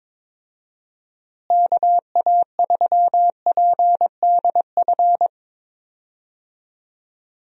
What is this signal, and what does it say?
Morse code, heard as a keyed tone, about 22 wpm, 700 Hz. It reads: KA3PDF
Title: USA Call Signs - 6 Characters - Repeated 3x - 22wpm